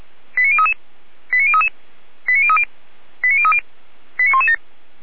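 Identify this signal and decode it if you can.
Five-tone selective call (ZVEI), heard as quick runs of five tones, tones of about 70 ms. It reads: 89033 89033 89030 89030 89117
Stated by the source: ZVEI five-tone selective call received on an SDR.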